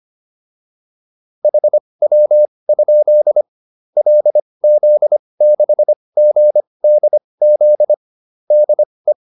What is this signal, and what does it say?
Morse code, heard as a keyed tone, about 25 wpm, spaced as for 20 wpm, 600 Hz. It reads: HW? LZ6GDZ DE
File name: QSO407 20